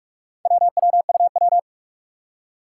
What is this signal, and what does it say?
Morse code, heard as a keyed tone, about 46 wpm, 700 Hz. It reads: WWUW
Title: Four Challenging Letter-Number Combinations 45wpm